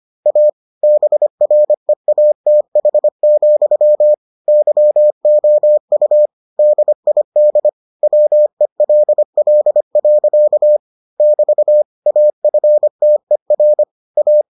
Morse code, heard as a keyed tone, about 25 wpm, 600 Hz. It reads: A BREATH, YOU DID WELL. <BT> AFTER A